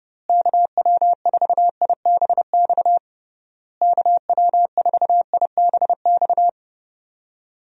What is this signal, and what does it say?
Morse code, heard as a keyed tone, about 30 wpm, 700 Hz. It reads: KW4IBX KW4IBX
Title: USA Call Signs - 6 Characters - Repeated 3x - 30wpm